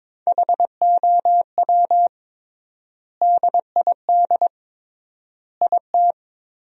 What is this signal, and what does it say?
Morse code, heard as a keyed tone, about 22 wpm, 700 Hz. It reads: HOW DID IT
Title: 3x Word-Spacing - Sentences from Top 300 Words - 22wpm